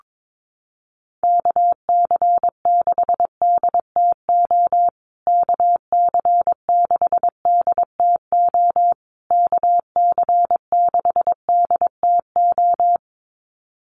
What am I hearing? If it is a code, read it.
Morse code, heard as a keyed tone, about 22 wpm, 700 Hz. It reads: KC6DTO KC6DTO KC6DTO
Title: USA Call Signs Repeated 3x 22wpm